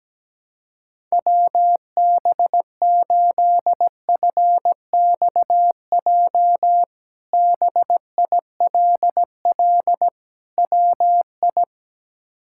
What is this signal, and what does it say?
Morse code, heard as a keyed tone, about 17 wpm, 700 Hz. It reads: WB8FXJ BILL WI
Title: K1USN SST QSO Practice - 17wpm